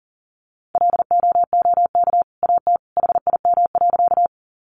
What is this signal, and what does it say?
Morse code, heard as a keyed tone, about 40 wpm, 700 Hz. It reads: LOOK AT HIM.
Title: Sentences from Top 100 Words 40wpm